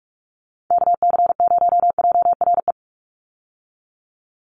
Morse code, heard as a keyed tone, about 48 wpm, 700 Hz. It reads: KC0JRE
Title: USA Call Signs 45wpm